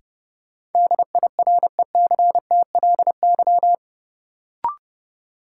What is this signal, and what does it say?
Morse code, heard as a keyed tone, about 32 wpm, 700 Hz. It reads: DIRECTLY
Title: Sentences from Top 800 Words - New Word Drill - 30wpm